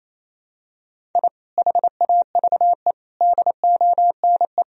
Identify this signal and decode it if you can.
Morse code, heard as a keyed tone, about 28 wpm, 700 Hz. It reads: I HAVE DONE
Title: Sentences from Top 1000 Words - Review Part 4 - 28wpm